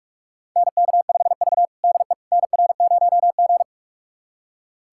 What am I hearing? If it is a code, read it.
Morse code, heard as a keyed tone, about 46 wpm, 700 Hz. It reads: NK5V DE NR0G
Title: USA Callsign DE Callsign - 4 Characters - 45wpm